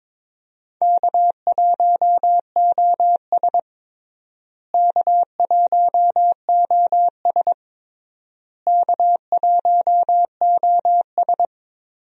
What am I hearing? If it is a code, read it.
Morse code, heard as a keyed tone, about 22 wpm, 700 Hz. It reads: K1OS K1OS K1OS
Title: USA Call Signs - 4 Characters - Repeated 3x - 22wpm